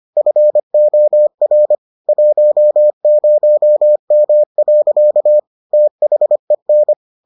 Morse code, heard as a keyed tone, about 25 wpm, 600 Hz. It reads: FOR 10M. THEN